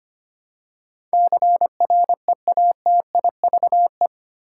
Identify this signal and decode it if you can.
Morse code, heard as a keyed tone, about 25 wpm, 700 Hz. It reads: CREATIVE
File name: Eight Letter Words - 25wpm